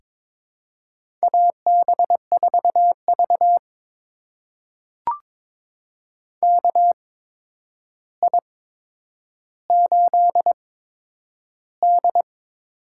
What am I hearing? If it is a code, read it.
Morse code, heard as a keyed tone, about 22 wpm, spaced as for 9 wpm, 700 Hz. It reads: AB4V KI8D
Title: USA Call Signs - 4 Characters - 8x Character Spacing - 22wpm